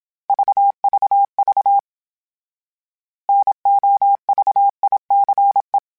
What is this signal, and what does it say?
Morse code, heard as a keyed tone, about 27 wpm, 800 Hz. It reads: VVV NOVICE
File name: fromSMs / FromSMs mp3s / Classes@25